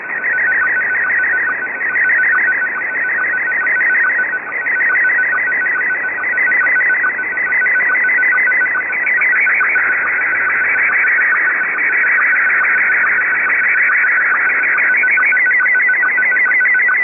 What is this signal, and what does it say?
SSTV (slow-scan television), Scottie 1.
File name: items / Httparchive.orgdetailsShortwavePirateRadioCont2 / AmphetamineRadio6925Usb2255z-2337z11-19-15